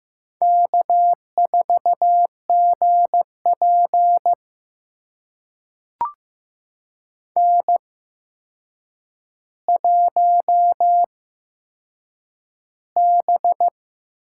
Morse code, heard as a keyed tone, about 15 wpm, spaced as for 7 wpm, 700 Hz. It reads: K4GP N1B
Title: USA Call Signs - 4 Characters - 8x Character Spacing - 15wpm